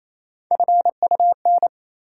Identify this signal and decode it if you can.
Morse code, heard as a keyed tone, about 28 wpm, 700 Hz. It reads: FUN
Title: Sentences from Top 700 Words - New Word Drill - 28wpm